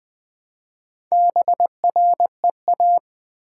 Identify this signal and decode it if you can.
Morse code, heard as a keyed tone, about 20 wpm, 700 Hz. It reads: BREA
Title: Build A Word - 2 Letters At A Time - 6-Letter Words - 20wpm